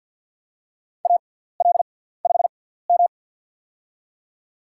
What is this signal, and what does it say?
Morse code, heard as a keyed tone, about 48 wpm, spaced as for 17 wpm, 700 Hz. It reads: AL5M
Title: USA Call Signs - 6x Character Spacing - 50wpm